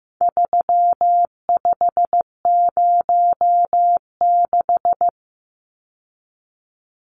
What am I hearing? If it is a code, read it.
Morse code, heard as a keyed tone, about 15 wpm, 700 Hz. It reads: 3506
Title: Numbers - 4-Digit 15wpm